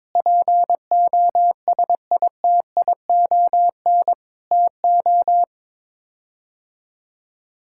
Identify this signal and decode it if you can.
Morse code, heard as a keyed tone, about 22 wpm, 700 Hz. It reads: POSITION TO
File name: practice / 22wpm / Sets of 7 Words - 22wpm